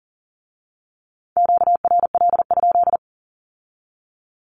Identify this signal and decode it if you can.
Morse code, heard as a keyed tone, about 42 wpm, 700 Hz. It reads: QRL?
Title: Q-Codes with Spoken Meaning - 40wpm